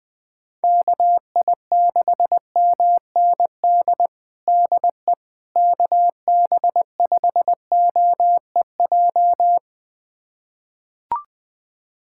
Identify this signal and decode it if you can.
Morse code, heard as a keyed tone, about 20 wpm, 700 Hz. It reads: KI6MND DE KB5OEJ
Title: USA Callsign DE Callsign 20wpm